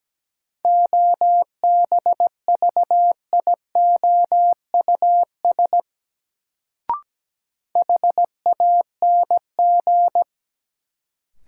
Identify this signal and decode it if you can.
Morse code, heard as a keyed tone, about 17 wpm, 700 Hz. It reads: OBVIOUS HANG
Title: Sentences from Top 1600 Words - New Word Drill - 17wpm